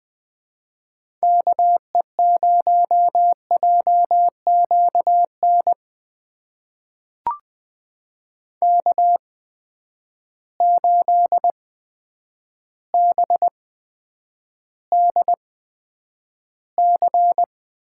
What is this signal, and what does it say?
Morse code, heard as a keyed tone, about 20 wpm, spaced as for 9 wpm, 700 Hz. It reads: KE0JQN K8BDC
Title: USA Call Signs - 8x Character Spacing - 20wpm